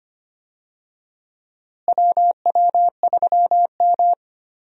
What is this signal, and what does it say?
Morse code, heard as a keyed tone, about 25 wpm, 700 Hz. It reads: WW3M
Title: USA Call Signs - 4 Characters - Repeated 2x - 25wpm